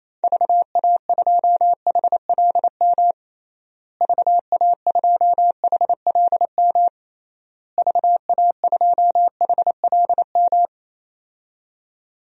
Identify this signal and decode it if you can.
Morse code, heard as a keyed tone, about 27 wpm, 700 Hz. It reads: VA2HLM VA2HLM VA2HLM
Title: DXCC-100 Call Signs - 6 Characters - Repeated 3x - 28wpm